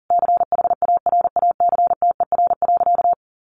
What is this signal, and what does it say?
Morse code, heard as a keyed tone, about 40 wpm, 700 Hz. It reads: CHARACTER.
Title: Sentences from Top 1500 Words 40wpm